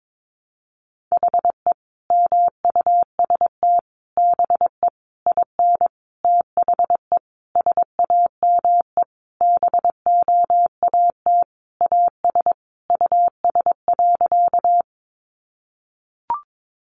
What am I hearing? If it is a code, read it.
Morse code, heard as a keyed tone, about 22 wpm, 700 Hz. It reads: HE MUST BE IN THE SAME BOAT AS US.
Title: Sentences from Top 500 Words 22wpm